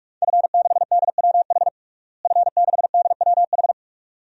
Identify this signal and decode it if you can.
Morse code, heard as a keyed tone, about 46 wpm, 700 Hz. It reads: F6DWH F6DWH
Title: DXCC-100 Call Signs - 5 Characters - Repeated 2x - 45wpm